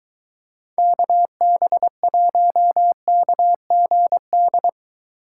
Morse code, heard as a keyed tone, about 23 wpm, 700 Hz. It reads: KB1KGD